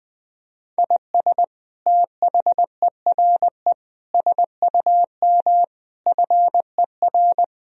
Morse code, heard as a keyed tone, about 20 wpm, 700 Hz. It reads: IS THERE SUM FER
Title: Sentences from Top 200 Words and Abbreviations 20wpm